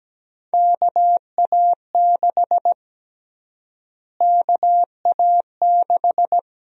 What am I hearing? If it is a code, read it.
Morse code, heard as a keyed tone, about 17 wpm, 700 Hz. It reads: KA6 KA6
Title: USA Call Sign Prefixes - Repeated 2x - 17wpm